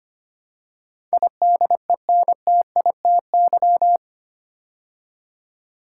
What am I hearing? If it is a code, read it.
Morse code, heard as a keyed tone, about 26 wpm, 690 Hz. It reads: IDENTITY
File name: 8 Letter Words - First 3 Letters Given - 25wpm